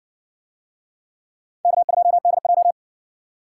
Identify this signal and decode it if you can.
Morse code, heard as a keyed tone, about 60 wpm, 700 Hz. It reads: K2DJ